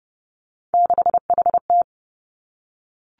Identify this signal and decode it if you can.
Morse code, heard as a keyed tone, about 30 wpm, 700 Hz. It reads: 6HT
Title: Three Challenging Letter-Number Combinations 30wpm